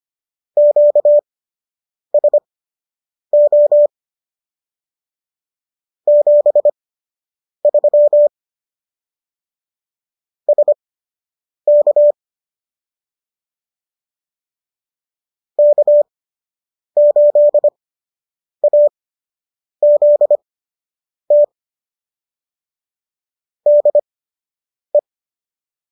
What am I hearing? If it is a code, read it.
Morse code, heard as a keyed tone, about 25 wpm, spaced as for 8 wpm, 590 Hz. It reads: QSO 73 SK K8AZT DE
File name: qso006 08